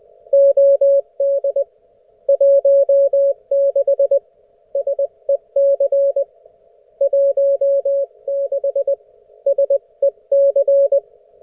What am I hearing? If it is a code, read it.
Morse code, heard as a keyed tone, about 20 wpm, 560 Hz. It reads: OD 16 SEC 16 SEC